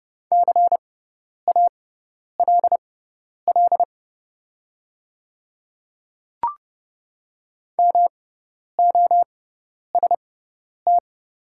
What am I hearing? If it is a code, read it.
Morse code, heard as a keyed tone, about 31 wpm, spaced as for 10 wpm, 690 Hz. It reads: CALL MOST
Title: 6x Character Spacing - Top 100 Words - 30wpm